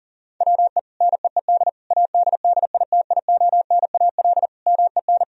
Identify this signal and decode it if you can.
Morse code, heard as a keyed tone, about 40 wpm, 700 Hz. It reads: WE NEED ADDITIONAL MEN
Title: Sentences from Top 1000 Words - Review Part 4 - 40wpm